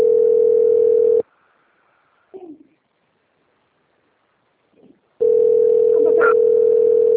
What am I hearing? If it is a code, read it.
Ringback tone (North America), 440 + 480 Hz.